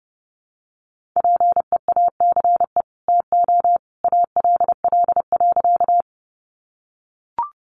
Morse code, heard as a keyed tone, about 30 wpm, 700 Hz. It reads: PEACE TO ALL.
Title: Sentences from Top 1500 Words 30wpm